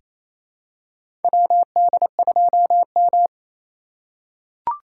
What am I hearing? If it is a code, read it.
Morse code, heard as a keyed tone, about 28 wpm, 700 Hz. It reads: WD2M
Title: USA Call Signs - 4 Characters - 3x Character Spacing - 28wpm